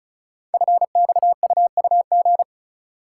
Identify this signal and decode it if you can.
Morse code, heard as a keyed tone, about 36 wpm, 700 Hz. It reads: FXUUG
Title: Code Groups - 5 Letters - 35wpm